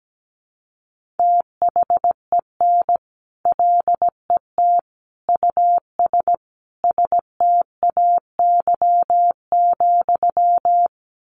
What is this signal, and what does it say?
Morse code, heard as a keyed tone, about 17 wpm, 700 Hz. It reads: THEN LET US STAY,
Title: Sentences from Top 1500 Words 17wpm